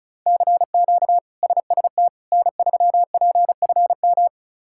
Morse code, heard as a keyed tone, about 36 wpm, 700 Hz. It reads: CQ SST N3PFM
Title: K1USN SST QSO Practice - 35wpm